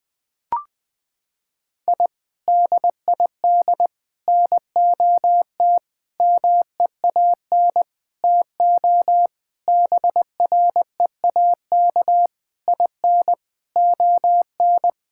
Morse code, heard as a keyed tone, about 19 wpm, 700 Hz. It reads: I DID NOT MEAN TO BREAK IN ON
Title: Sentences from Top 1000 Words - Review Part 6 - 20wpm